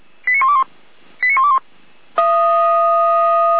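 Five-tone selective call (ZVEI), heard as quick runs of five tones, tones of about 70 ms. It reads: 89121 89121
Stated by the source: ZVEI five-tone selective call received by SDR.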